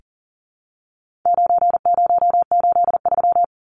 Morse code, heard as a keyed tone, about 40 wpm, 700 Hz. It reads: 9083
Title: Numbers - 4-Digit 40wpm